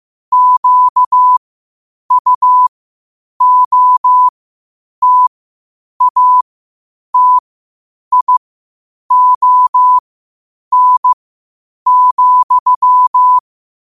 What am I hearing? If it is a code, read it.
Morse code, heard as a keyed tone, about 15 wpm, spaced as for 8 wpm, 1000 Hz. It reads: QUOTATION,
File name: 8wpm